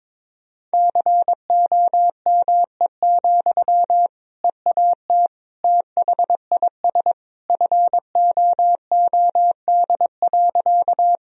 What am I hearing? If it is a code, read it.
Morse code, heard as a keyed tone, about 22 wpm, 700 Hz. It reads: COME, EAT THIS FOOD.
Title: Sentences from Top 1000 Words - Review Part 1 - 22wpm